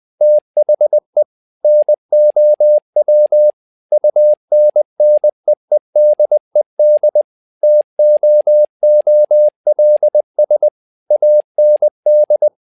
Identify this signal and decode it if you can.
Morse code, heard as a keyed tone, about 20 wpm, 600 Hz. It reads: THE NOW UNNEEDED TOOLS AND
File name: tower trouble 20wpm